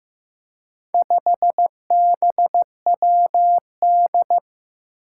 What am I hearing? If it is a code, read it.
Morse code, heard as a keyed tone, about 15 wpm, 700 Hz. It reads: 5BWD
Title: Four Challenging Letter-Number Combinations 15wpm